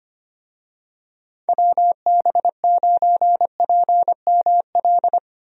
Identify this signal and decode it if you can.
Morse code, heard as a keyed tone, about 25 wpm, 700 Hz. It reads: WB9PML
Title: USA Call Signs - 6 Characters - Repeated 2x - 25wpm